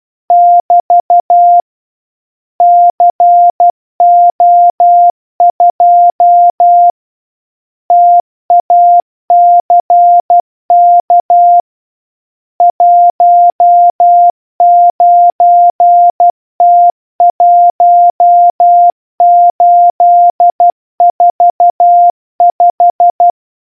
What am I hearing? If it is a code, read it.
Morse code, heard as a keyed tone, about 12 wpm, 700 Hz. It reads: <BT> CO2 TACK 19T1845